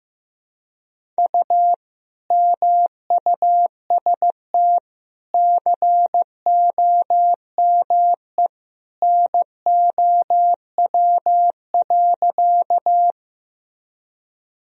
Morse code, heard as a keyed tone, about 15 wpm, 700 Hz. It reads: U MUST COME NOW.
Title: Sentences from Top 200 Words and Abbreviations 15wpm